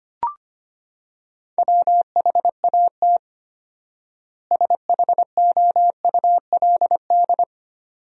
Morse code, heard as a keyed tone, about 25 wpm, 700 Hz. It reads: WHAT SHOULD
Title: 4x Word-Spacing - Sentences from Top 300 Words - 25wpm